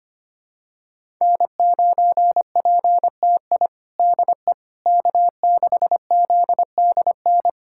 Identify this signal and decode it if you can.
Morse code, heard as a keyed tone, about 25 wpm, 700 Hz. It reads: N9PTI DE K6ZDN
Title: USA Callsign DE Callsign - 5 Characters - 25wpm